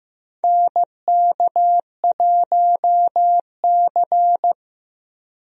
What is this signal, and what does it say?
Morse code, heard as a keyed tone, about 15 wpm, 700 Hz. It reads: NK1C
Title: USA Call Signs - 4 Characters - Repeated 2x - 15wpm